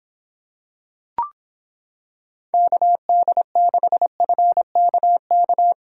Morse code, heard as a keyed tone, about 26 wpm, 700 Hz. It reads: KD6FKK